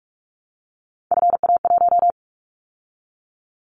Morse code, heard as a keyed tone, about 46 wpm, 700 Hz. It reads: FA1
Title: Callsign Prefixes for Top 25 DXCC - 45wpm